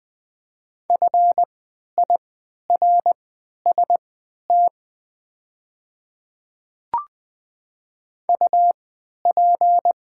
Morse code, heard as a keyed tone, about 20 wpm, spaced as for 11 wpm, 700 Hz. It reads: FIRST UP
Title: 3x Character Spacing - Top 100 Words - 20wpm